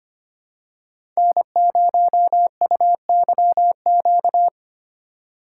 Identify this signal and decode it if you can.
Morse code, heard as a keyed tone, about 25 wpm, 700 Hz. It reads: N0UYQ
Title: USA Call Signs - 5 Characters - 3x Character Spacing - 25wpm